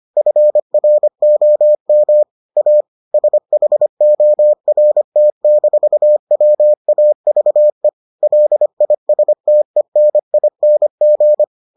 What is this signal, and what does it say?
Morse code, heard as a keyed tone, about 25 wpm, 600 Hz. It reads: FROM A SHORT-WAVE LISTENING